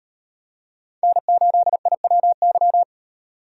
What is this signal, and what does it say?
Morse code, heard as a keyed tone, about 38 wpm, 700 Hz. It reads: N8IWY